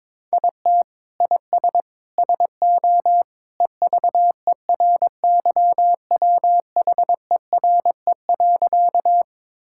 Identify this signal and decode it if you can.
Morse code, heard as a keyed tone, about 22 wpm, 700 Hz. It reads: IT IS SO EVERYWHERE.
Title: Sentences from Top 1600 Words 22wpm